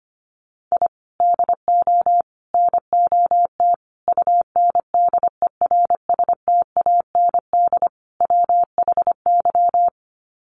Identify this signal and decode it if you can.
Morse code, heard as a keyed tone, about 25 wpm, 700 Hz. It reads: I DO NOT UNDERSTAND WHY
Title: Sets of 5 Words Encore 25wpm